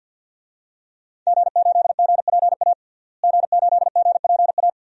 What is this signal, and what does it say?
Morse code, heard as a keyed tone, about 51 wpm, 700 Hz. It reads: G8GPA G8GPA
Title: DXCC-100 Call Signs - 5 Characters - Repeated 2x - 50wpm